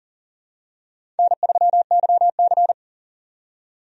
Morse code, heard as a keyed tone, about 40 wpm, 700 Hz. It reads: N3YC
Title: USA Call Signs - 4 Characters - 40wpm